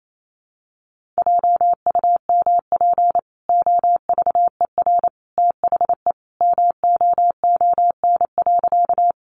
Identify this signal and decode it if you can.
Morse code, heard as a keyed tone, about 28 wpm, 700 Hz. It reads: JUMP OVER THE MOON.
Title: Sentences from Top 1300 Words 28wpm